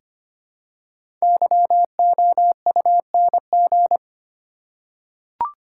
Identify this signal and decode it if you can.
Morse code, heard as a keyed tone, about 25 wpm, 700 Hz. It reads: YOUNG
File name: 5 Letter Words - First 3 Letters Given - 25wpm